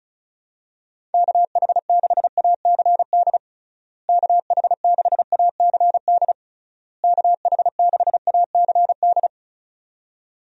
Morse code, heard as a keyed tone, about 36 wpm, 700 Hz. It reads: KH6ACD KH6ACD KH6ACD
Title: USA Call Signs - 6 Characters - Repeated 3x - 35wpm